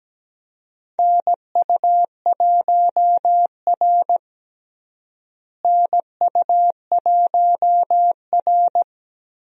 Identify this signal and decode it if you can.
Morse code, heard as a keyed tone, about 17 wpm, 700 Hz. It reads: NU1R NU1R
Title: USA Call Signs - 4 Characters - Repeated 2x - 17wpm